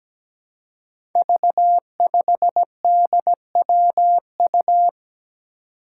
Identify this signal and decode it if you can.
Morse code, heard as a keyed tone, about 17 wpm, 700 Hz. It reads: V5DWU
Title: Five Challenging Letter-Number Combinations 17wpm